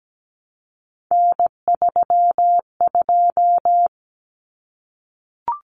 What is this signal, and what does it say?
Morse code, heard as a keyed tone, about 17 wpm, 700 Hz. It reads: N32
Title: Common Cut Numbers - 3-Digit 17wpm